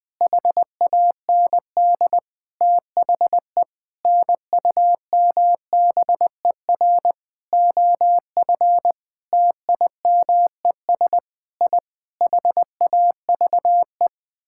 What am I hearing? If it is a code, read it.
Morse code, heard as a keyed tone, about 20 wpm, 700 Hz. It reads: HAND THE NUMBER OF TIMES I HAVE
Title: Sentences with 8-Word Phrases - Spoken Beginning - 20wpm